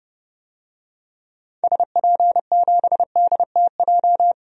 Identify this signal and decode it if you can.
Morse code, heard as a keyed tone, about 31 wpm, 700 Hz. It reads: SP7DTJ
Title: DXCC-100 Call Signs - 6 Characters - 30wpm